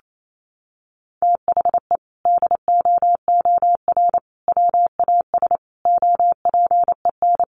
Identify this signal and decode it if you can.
Morse code, heard as a keyed tone, about 28 wpm, 700 Hz. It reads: THE DOOR WAS OPEN
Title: Sentences from Top 1300 Words 28wpm